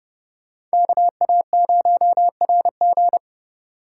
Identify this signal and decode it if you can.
Morse code, heard as a keyed tone, about 37 wpm, 700 Hz. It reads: KA0RG
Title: Speed Racing - USA 2x2 Call Signs 30wpm